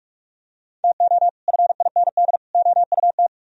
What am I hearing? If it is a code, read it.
Morse code, heard as a keyed tone, about 46 wpm, 700 Hz. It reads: TO FIND OUT